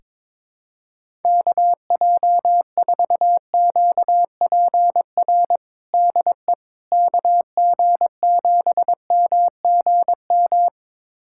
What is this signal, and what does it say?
Morse code, heard as a keyed tone, about 22 wpm, 700 Hz. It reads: KJ4QPR DE KG7MGM
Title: USA Callsign DE Callsign 22wpm